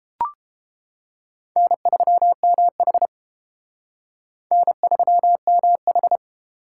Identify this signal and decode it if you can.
Morse code, heard as a keyed tone, about 30 wpm, 700 Hz. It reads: N3MH N3MH